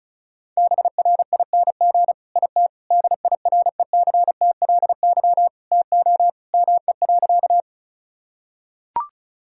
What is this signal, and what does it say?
Morse code, heard as a keyed tone, about 34 wpm, 700 Hz. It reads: BRING IT DIRECTLY TO ME.
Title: Sentences from Top 1000 Words - Review Part 5 - 35wpm